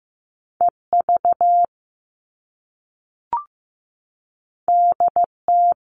Morse code, heard as a keyed tone, about 15 wpm, 700 Hz. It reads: EV DT
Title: All Cut Numbers - 2-Digit 15wpm